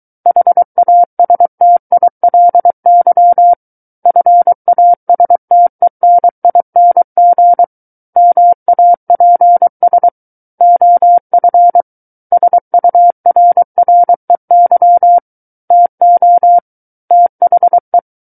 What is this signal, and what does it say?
Morse code, heard as a keyed tone, about 23 wpm, 700 Hz. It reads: HASTILY FASTENING MAPS OF SURREY TO THE
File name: Book One / 14-Chapter 14 - 23 WPM